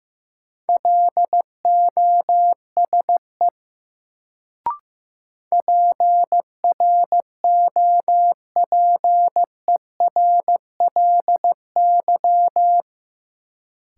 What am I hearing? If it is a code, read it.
Morse code, heard as a keyed tone, about 15 wpm, 700 Hz. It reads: LOSE PROPERLY
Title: Sentences from Top 1100 Words - New Word Drill Alt - 15wpm